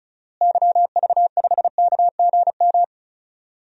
Morse code, heard as a keyed tone, about 36 wpm, 700 Hz. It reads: YV5KGM
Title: DXCC-100 Call Signs - 6 Characters - Repeated 3x - 35wpm